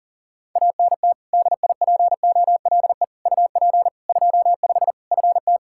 Morse code, heard as a keyed tone, about 40 wpm, 700 Hz. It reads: ANT DIPOLE UP 25 FT
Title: Practice QSO - Exchange 2 - Info - 40wpm